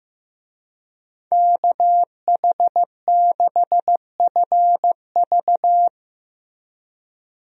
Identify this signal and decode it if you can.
Morse code, heard as a keyed tone, about 15 wpm, 700 Hz. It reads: KH6FV
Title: USA Call Signs - 5 Characters - Repeated 2x - 15wpm